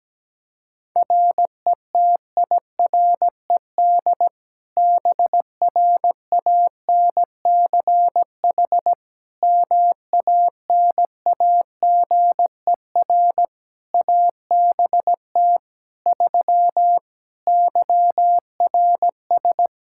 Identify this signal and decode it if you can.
Morse code, heard as a keyed tone, about 17 wpm, 700 Hz. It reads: RETIRED BRANCH MANAGER ABT 3 YRS
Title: Practice QSO - Exchange 3 - More Info - 17wpm